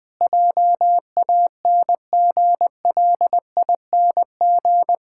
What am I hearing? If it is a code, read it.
Morse code, heard as a keyed tone, about 20 wpm, 700 Hz. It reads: JANGLING
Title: Pangram Challenge v2 - 20wpm